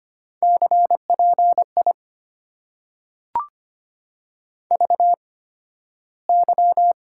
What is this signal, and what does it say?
Morse code, heard as a keyed tone, about 25 wpm, spaced as for 13 wpm, 690 Hz. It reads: CPI VY
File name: Top 90 QSO Elements - 8x Character Spacing - 25wpm